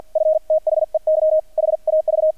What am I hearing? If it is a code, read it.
Morse code, heard as a keyed tone, about 50 wpm, 630 Hz. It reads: F THEY HAV
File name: archived / CRSnet